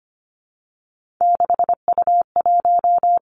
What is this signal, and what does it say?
Morse code, heard as a keyed tone, about 25 wpm, 700 Hz. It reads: 6U1
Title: Three Challenging Letter-Number Combinations 25wpm